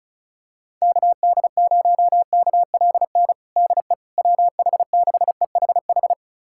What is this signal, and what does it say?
Morse code, heard as a keyed tone, about 36 wpm, 700 Hz. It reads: KD0KLN DE WH6EHH
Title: USA Callsign DE Callsign - 6 Characters - 35wpm